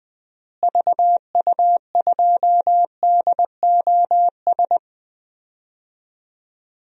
Morse code, heard as a keyed tone, about 20 wpm, 700 Hz. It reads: VU2DOS
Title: DXCC-100 Call Signs - 6 Characters - Repeated 2x - 20wpm